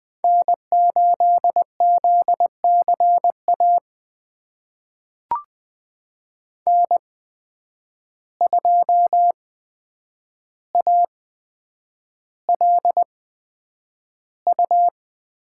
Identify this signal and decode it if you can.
Morse code, heard as a keyed tone, about 20 wpm, spaced as for 9 wpm, 700 Hz. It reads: N8ZCA N2ALU
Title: USA Call Signs - 5 Characters - 8x Character Spacing - 20wpm